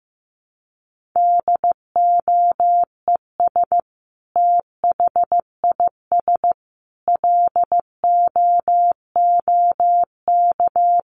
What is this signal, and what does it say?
Morse code, heard as a keyed tone, about 15 wpm, 700 Hz. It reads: DOES THIS LOOK
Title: Sentences from Top 1500 Words 15wpm